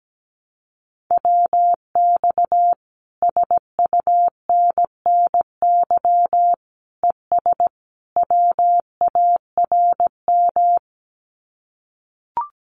Morse code, heard as a keyed tone, about 17 wpm, 700 Hz. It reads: WX SUNNY ES WARM
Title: Common QSO Phrases Encore 17wpm